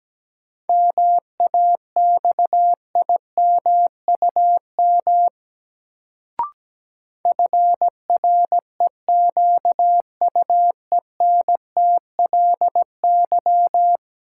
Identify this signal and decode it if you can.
Morse code, heard as a keyed tone, about 17 wpm, 700 Hz. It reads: MAXIMUM FREQUENTLY
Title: Sentences from Top 1300 Words - New Word Drill Alt - 17wpm